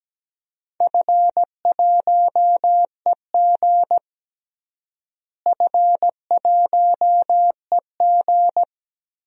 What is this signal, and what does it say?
Morse code, heard as a keyed tone, about 17 wpm, 700 Hz. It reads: F1EG F1EG
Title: DXCC-100 Call Signs - 4 Characters - Repeated 2x - 17wpm